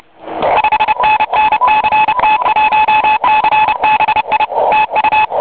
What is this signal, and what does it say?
Morse code, heard as a keyed tone, about 29 wpm, 2600 Hz, with radio noise. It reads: 5NN CT1CBI TU
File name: oy 2003 / real-audio / 12m-oy7et-cw-3min